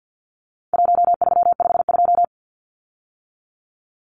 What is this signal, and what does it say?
Morse code, heard as a keyed tone, about 50 wpm, 700 Hz. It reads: 1352
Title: Numbers - Contest Style - 4-Digit 50wpm